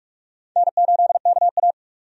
Morse code, heard as a keyed tone, about 45 wpm, 700 Hz. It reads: N8KA
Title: USA Call Signs - 4 Characters - 45wpm